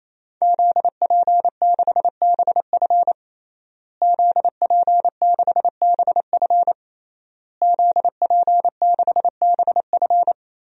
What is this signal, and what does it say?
Morse code, heard as a keyed tone, about 28 wpm, 700 Hz. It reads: ZP6BF ZP6BF ZP6BF
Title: DXCC-100 Call Signs - 5 Characters - Repeated 3x - 28wpm